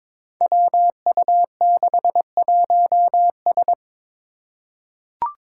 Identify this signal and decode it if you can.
Morse code, heard as a keyed tone, about 22 wpm, 700 Hz. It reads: WU61S
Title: Five Challenging Letter-Number Combinations 22wpm